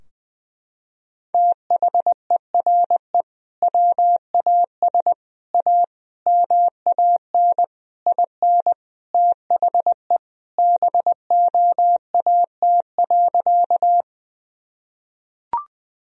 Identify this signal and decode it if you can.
Morse code, heard as a keyed tone, about 20 wpm, 700 Hz. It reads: THERE WAS A MAN IN THE BOAT.
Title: Sentences from Top 1000 Words - Review Part 3 - 20wpm